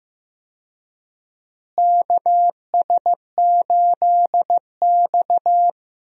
Morse code, heard as a keyed tone, about 15 wpm, 700 Hz. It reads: KS8X
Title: USA Call Signs - 4 Characters - Repeated 3x - 15wpm